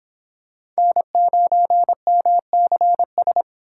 Morse code, heard as a keyed tone, about 25 wpm, 700 Hz. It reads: N9MCS